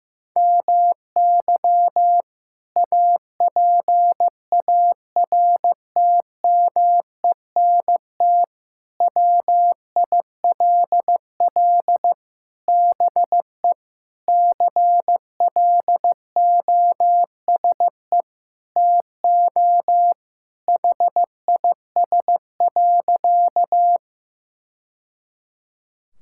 Morse code, heard as a keyed tone, about 15 wpm, 700 Hz. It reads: MY APARTMENT WILL BE CLOSE TO HIS.
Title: Sentences from Top 1000 Words - Review Part 6 - 15wpm